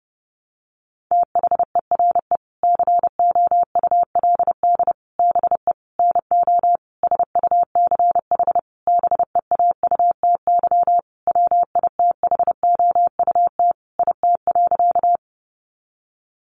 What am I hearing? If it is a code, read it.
Morse code, heard as a keyed tone, about 30 wpm, 700 Hz. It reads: THERE COULD BE NO SUCH BEAUTY WITHOUT IT.
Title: Sentences from Top 600 Words 30wpm